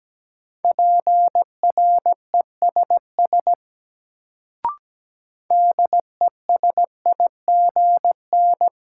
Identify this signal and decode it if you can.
Morse code, heard as a keyed tone, about 17 wpm, 700 Hz. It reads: PRESS DESIGN
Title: Sentences from Top 800 Words - New Word Drill - 17wpm